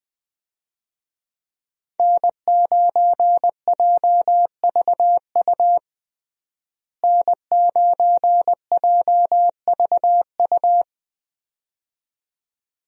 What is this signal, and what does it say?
Morse code, heard as a keyed tone, about 20 wpm, 700 Hz. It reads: N9JVU N9JVU
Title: USA Call Signs - 5 Characters - Repeated 2x - 20wpm